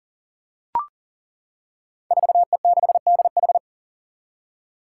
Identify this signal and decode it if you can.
Morse code, heard as a keyed tone, about 37 wpm, 700 Hz. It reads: 4E6DH
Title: Five Challenging Letter-Number Combinations 40wpm